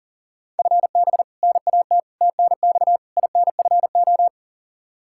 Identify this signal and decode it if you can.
Morse code, heard as a keyed tone, about 40 wpm, 700 Hz. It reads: FB NAT TNX INFO
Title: Practice QSO - Exchange 3 - More Info - 40wpm